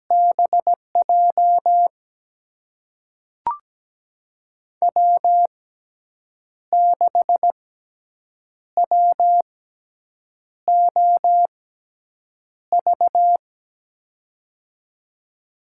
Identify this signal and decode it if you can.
Morse code, heard as a keyed tone, about 17 wpm, spaced as for 7 wpm, 700 Hz. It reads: BJ W6WOV
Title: USA Call Signs - 5 Characters - 6x Character Spacing - 17wpm